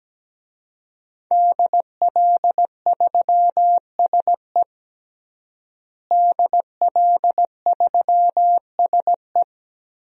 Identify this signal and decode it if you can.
Morse code, heard as a keyed tone, about 17 wpm, 700 Hz. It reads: DL3SE DL3SE
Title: DXCC-100 Call Signs - 5 Characters - Repeated 2x - 17wpm